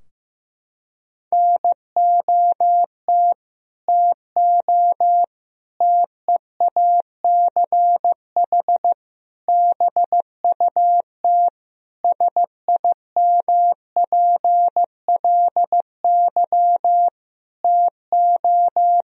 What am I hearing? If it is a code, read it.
Morse code, heard as a keyed tone, about 15 wpm, 700 Hz. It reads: NOT TO TEACH BUT SIMPLY TO
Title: Sentences from Top 1000 Words - Review Part 4 - 15wpm